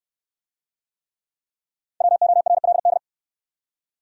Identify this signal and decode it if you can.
Morse code, heard as a keyed tone, about 68 wpm, 700 Hz. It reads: WORLD